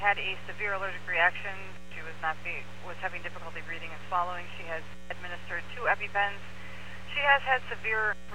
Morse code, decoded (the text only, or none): none